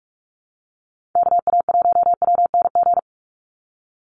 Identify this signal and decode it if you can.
Morse code, heard as a keyed tone, about 44 wpm, 700 Hz. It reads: KA1WNG
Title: USA Call Signs 45wpm